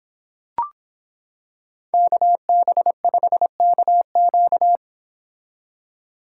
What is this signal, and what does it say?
Morse code, heard as a keyed tone, about 25 wpm, 700 Hz. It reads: KB5KQ